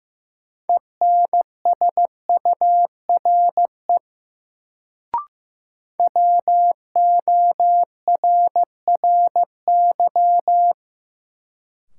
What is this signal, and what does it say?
Morse code, heard as a keyed tone, about 15 wpm, 700 Hz. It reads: ENSURE WORRY
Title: Sentences from Top 1100 Words - New Word Drill Alt - 15wpm